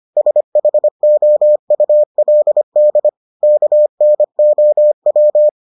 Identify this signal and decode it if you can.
Morse code, heard as a keyed tone, about 25 wpm, 600 Hz. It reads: SHOULD KNOW